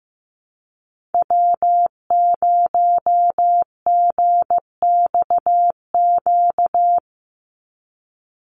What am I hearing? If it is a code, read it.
Morse code, heard as a keyed tone, about 15 wpm, 700 Hz. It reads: W0GXQ
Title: Call Signs Repeated 3x - 15wpm